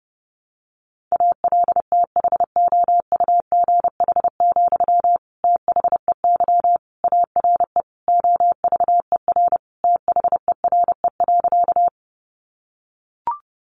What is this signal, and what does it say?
Morse code, heard as a keyed tone, about 30 wpm, 700 Hz. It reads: ALTHOUGH, THEY ARE OVER THERE.
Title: Sentences from Top 600 Words 30wpm